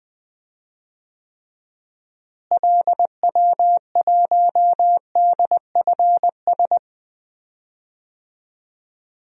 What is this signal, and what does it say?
Morse code, heard as a keyed tone, about 21 wpm, 700 Hz. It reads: LW1DFS
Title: DXCC-100 Call Signs - 6 Characters - Repeated 2x - 20wpm